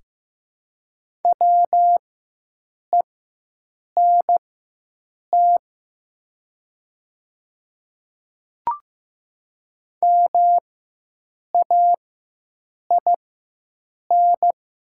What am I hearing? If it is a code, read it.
Morse code, heard as a keyed tone, about 15 wpm, spaced as for 7 wpm, 700 Hz. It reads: WENT MAIN
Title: 4x Character Spacing - Top 500 Words - 15wpm